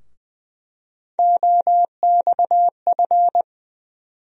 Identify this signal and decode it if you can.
Morse code, heard as a keyed tone, about 20 wpm, 700 Hz. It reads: OXF
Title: USA Call Sign Postfixes - 3 Letters 20wpm